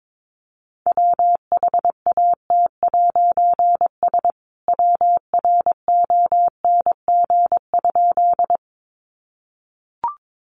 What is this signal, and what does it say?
Morse code, heard as a keyed tone, about 22 wpm, 700 Hz. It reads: WHAT'S WRONG?
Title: A Sound Of Thunder 22wpm